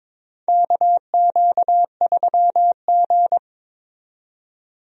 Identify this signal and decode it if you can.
Morse code, heard as a keyed tone, about 22 wpm, 700 Hz. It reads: KQ3G
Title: USA Call Signs - 4 Characters - 22wpm